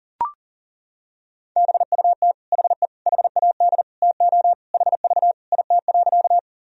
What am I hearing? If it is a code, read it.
Morse code, heard as a keyed tone, about 37 wpm, 700 Hz. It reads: BUT HE HAD TO HV IT.
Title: Sentences from Top 100 Words and Abbreviations 40wpm